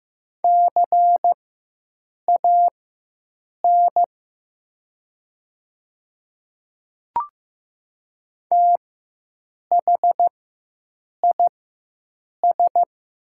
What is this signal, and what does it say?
Morse code, heard as a keyed tone, about 15 wpm, spaced as for 7 wpm, 700 Hz. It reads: CAN THIS
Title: 4x Character Spacing - Top 100 Words - 15wpm